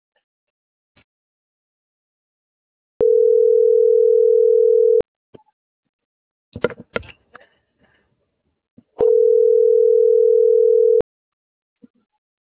Ringback tone (North America), 440 + 480 Hz.